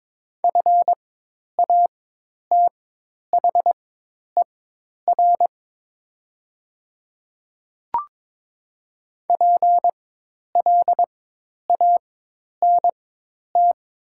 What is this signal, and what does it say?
Morse code, heard as a keyed tone, about 22 wpm, spaced as for 10 wpm, 700 Hz. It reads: FATHER PLANT
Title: 4x Character Spacing - Top 300 Words - 22wpm